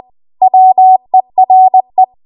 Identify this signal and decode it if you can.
Morse code, heard as a keyed tone, about 20 wpm, 750 Hz. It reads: WERE
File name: Archive / 20 WPM / 20WPM